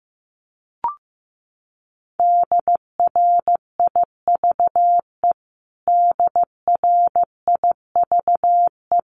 Morse code, heard as a keyed tone, about 15 wpm, 700 Hz. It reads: DRIVE DRIVE
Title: Top 500 Words Repeated 15wpm